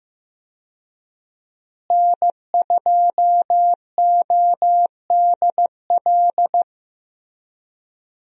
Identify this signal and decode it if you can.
Morse code, heard as a keyed tone, about 15 wpm, 700 Hz. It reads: N2ODL
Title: USA Call Signs - 5 Characters - Repeated 2x - 15wpm